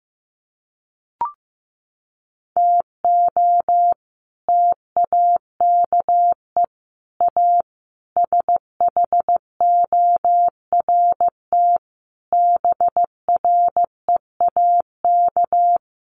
Morse code, heard as a keyed tone, about 15 wpm, 700 Hz. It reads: TO TAKE A SHORT BREAK
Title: Sets of 5 Words Encore2 15wpm